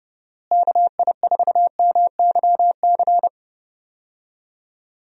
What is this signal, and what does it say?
Morse code, heard as a keyed tone, about 32 wpm, 700 Hz. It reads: KI4MYC
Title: USA Call Signs - 6 Characters - Repeated 3x - 30wpm